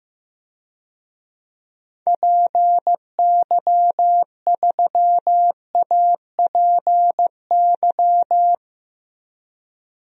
Morse code, heard as a keyed tone, about 15 wpm, 700 Hz. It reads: PY3APY
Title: DXCC-100 Call Signs - 6 Characters - 15wpm